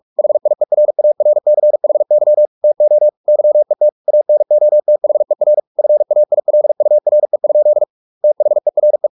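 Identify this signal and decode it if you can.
Morse code, heard as a keyed tone, about 44 wpm, 600 Hz. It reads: HIERARCHY TO YET ANOTHER FAILURE? THERE
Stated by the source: Morse practice at 45 wpm, the novel Never by Ken Follett in English, from CW med Gnister.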